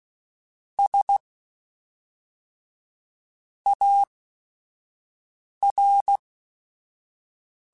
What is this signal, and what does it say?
Morse code, heard as a keyed tone, about 16 wpm, 800 Hz. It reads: S A R
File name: Technical Subjects / Technical Subjects-mp3s / Reactance @5